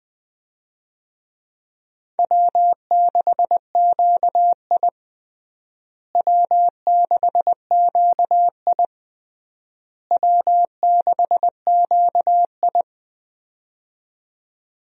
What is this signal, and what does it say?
Morse code, heard as a keyed tone, about 20 wpm, 700 Hz. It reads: W6QI W6QI W6QI
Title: USA Call Signs - 4 Characters - Repeated 3x - 20wpm